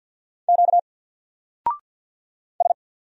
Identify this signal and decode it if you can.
Morse code, heard as a keyed tone, about 48 wpm, 700 Hz. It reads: Q S
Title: Single Letter-Number Alt 50wpm